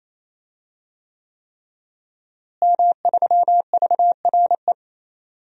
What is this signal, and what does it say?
Morse code, heard as a keyed tone, about 28 wpm, 700 Hz. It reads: M3VRE
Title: DXCC-100 Call Signs - 5 Characters - Repeated 3x - 28wpm